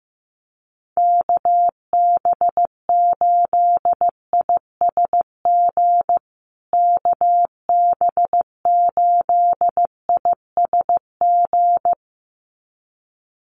Morse code, heard as a keyed tone, about 15 wpm, 700 Hz. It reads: KB8ISG KB8ISG
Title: USA Call Signs Repeated 2x 15wpm